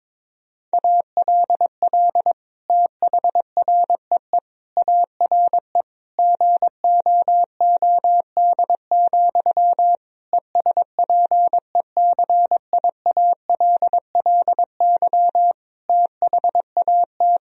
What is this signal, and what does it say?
Morse code, heard as a keyed tone, about 22 wpm, 700 Hz. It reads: ALL THREE ARE GOOD, ESPECIALLY THAT
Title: Sentences from Top 1000 Words - Review Part 4 - 22wpm